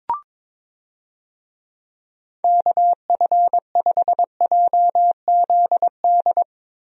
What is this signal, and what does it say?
Morse code, heard as a keyed tone, about 22 wpm, 700 Hz. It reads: KF5JZD
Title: USA Call Signs - 6 Characters - Repeated 2x - 22wpm